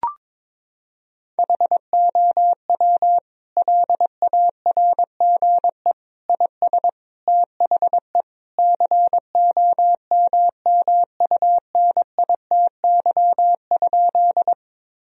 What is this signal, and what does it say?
Morse code, heard as a keyed tone, about 21 wpm, 700 Hz. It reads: HOW LARGE IS THE COMMUNITY?
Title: Sentences from Top 1000 Words - Review Part 4 - 22wpm